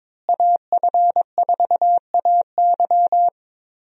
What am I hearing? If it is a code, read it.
Morse code, heard as a keyed tone, about 22 wpm, 700 Hz. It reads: AF4AY
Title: USA Call Signs - 5 Characters - Repeated 3x - 22wpm